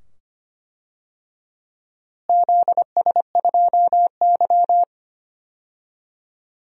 Morse code, heard as a keyed tone, about 25 wpm, 700 Hz. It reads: ZS2Y
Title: DXCC-100 Call Signs - 4 Characters - Repeated 2x - 25wpm